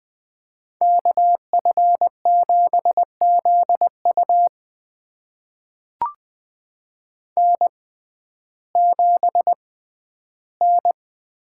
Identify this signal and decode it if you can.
Morse code, heard as a keyed tone, about 21 wpm, spaced as for 12 wpm, 700 Hz. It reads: KF7ZU N7N
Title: USA Call Signs - 5 Characters - 6x Character Spacing - 20wpm